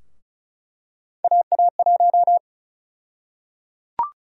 Morse code, heard as a keyed tone, about 36 wpm, 700 Hz. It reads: AA1
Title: USA Call Sign Prefixes - Repeated 2x - 35wpm